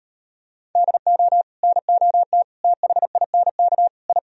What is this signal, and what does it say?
Morse code, heard as a keyed tone, about 36 wpm, 700 Hz. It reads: DO NOT THINK I